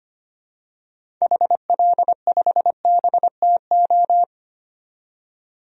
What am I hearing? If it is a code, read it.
Morse code, heard as a keyed tone, about 25 wpm, 700 Hz. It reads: HL5BTO
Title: DXCC-100 Call Signs - 6 Characters - 25wpm